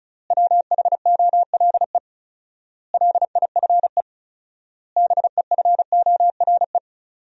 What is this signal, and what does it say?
Morse code, heard as a keyed tone, about 36 wpm, 700 Hz. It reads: WHOLE LIFE BEFORE
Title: 4x Word-Spacing - Sentences from Top 400 Words - 35wpm